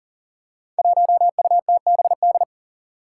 Morse code, heard as a keyed tone, about 41 wpm, 700 Hz. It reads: 1UTBD
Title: Five Challenging Letter-Number Combinations 40wpm